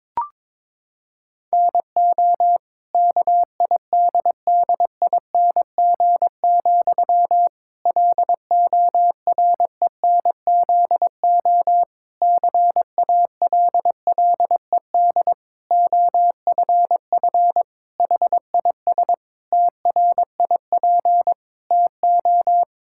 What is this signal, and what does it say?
Morse code, heard as a keyed tone, about 22 wpm, 700 Hz. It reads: NO KIDDING, LORENZO CALLED OFF HIS TRIP TO
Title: Pangram Challenge v2 - 22wpm